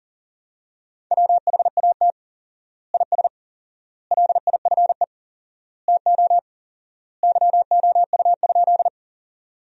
Morse code, heard as a keyed tone, about 37 wpm, 700 Hz. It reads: WHAT IS LIFE TO YOU?
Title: 4x Word-Spacing - Sentences from Top 300 Words - 40wpm